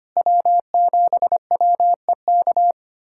Morse code, heard as a keyed tone, about 25 wpm, 700 Hz. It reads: W7WEK
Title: USA Callsign DE Callsign - 5 Characters - 25wpm